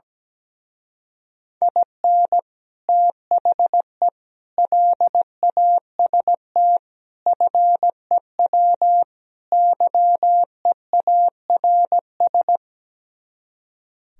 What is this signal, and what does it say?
Morse code, heard as a keyed tone, about 17 wpm, 700 Hz. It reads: IN THE LAST FEW YEARS
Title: Speed-Racing - Sets of 5 Words 17wpm